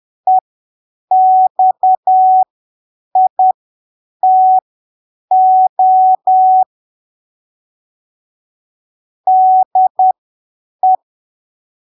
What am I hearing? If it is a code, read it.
Morse code, heard as a keyed tone, about 10 wpm, spaced as for 7 wpm, 750 Hz. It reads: EXITO DE
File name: arquivos / texto / texto 09 10ppm